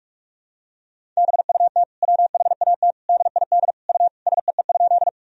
Morse code, heard as a keyed tone, about 46 wpm, 700 Hz. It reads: BUT WHAT DID U SEE?
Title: Sentences from Top 100 Words and Abbreviations 45wpm